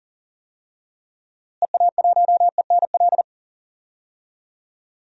Morse code, heard as a keyed tone, about 37 wpm, 700 Hz. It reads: EA1ENL
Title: DXCC-100 Call Signs - 6 Characters - Repeated 3x - 40wpm